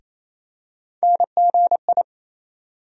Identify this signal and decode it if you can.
Morse code, heard as a keyed tone, about 28 wpm, 700 Hz. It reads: NGI
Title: USA Call Sign Postfixes - 3 Letters 28wpm